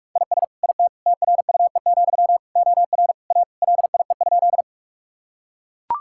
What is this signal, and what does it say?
Morse code, heard as a keyed tone, about 43 wpm, 700 Hz. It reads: IS IT TRUE, OR A LIE?
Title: Sentences from Top 1600 Words 45wpm